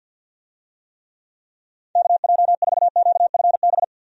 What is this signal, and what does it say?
Morse code, heard as a keyed tone, about 50 wpm, 700 Hz. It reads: KJ4QFB
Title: USA Call Signs - 6 Characters - 50wpm